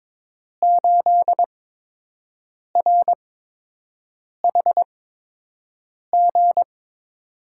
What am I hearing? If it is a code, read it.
Morse code, heard as a keyed tone, about 22 wpm, spaced as for 6 wpm, 700 Hz. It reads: 8RHG
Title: USA Call Signs - 5 Characters - 8x Character Spacing - 22wpm